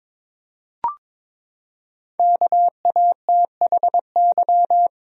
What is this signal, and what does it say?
Morse code, heard as a keyed tone, about 22 wpm, 700 Hz. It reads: KATHY
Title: USA Common Names - 22wpm